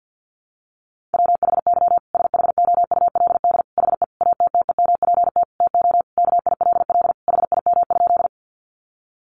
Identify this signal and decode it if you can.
Morse code, heard as a keyed tone, about 50 wpm, 700 Hz. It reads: WHY SHOULD HE ATTEMPT TO KILL HIM?
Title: Sentences from Top 1400 Words 50wpm